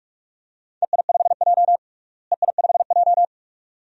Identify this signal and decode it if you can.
Morse code, heard as a keyed tone, about 41 wpm, 700 Hz. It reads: EI5J EI5J
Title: DXCC-100 Call Signs - 4 Characters - Repeated 2x - 45wpm